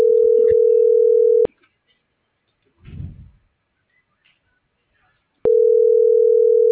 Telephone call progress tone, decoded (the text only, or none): ringback tone